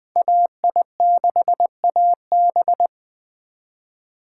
Morse code, heard as a keyed tone, about 24 wpm, 700 Hz. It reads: AI6AB
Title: Speed Racing - USA 2x2 Call Signs 20wpm